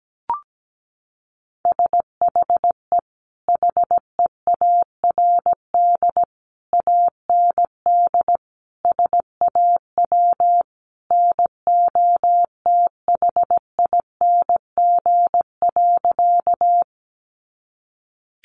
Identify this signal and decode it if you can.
Morse code, heard as a keyed tone, about 17 wpm, 700 Hz. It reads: SHE HEARD AND SAW NOTHING.
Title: Sentences from Top 500 Words 17wpm